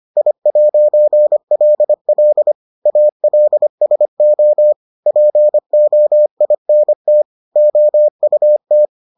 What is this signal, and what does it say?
Morse code, heard as a keyed tone, about 25 wpm, 600 Hz. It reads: I'LL ALSO POINT OUT